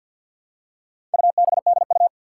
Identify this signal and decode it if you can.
Morse code, heard as a keyed tone, about 50 wpm, 700 Hz. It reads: UBDU
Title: Four Challenging Letter-Number Combinations 50wpm